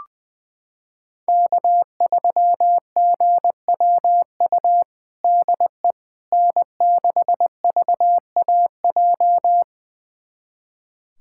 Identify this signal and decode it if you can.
Morse code, heard as a keyed tone, about 20 wpm, 700 Hz. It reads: K3GWU DE N6VAJ
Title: USA Callsign DE Callsign - 5 Characters - 20wpm